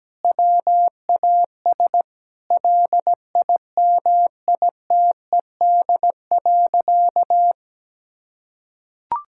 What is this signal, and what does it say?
Morse code, heard as a keyed tone, about 17 wpm, 700 Hz. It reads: WAS LIMITED.
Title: Sentences from Top 1000 Words - Review Part 6 - 17wpm